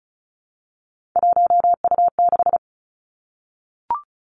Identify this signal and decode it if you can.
Morse code, heard as a keyed tone, about 36 wpm, 700 Hz. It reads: 1U6
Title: Three Challenging Letter-Number Combinations 35wpm